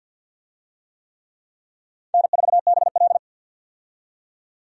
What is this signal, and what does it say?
Morse code, heard as a keyed tone, about 53 wpm, 700 Hz. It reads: N4BL
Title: USA Call Signs - 4 Characters - 50wpm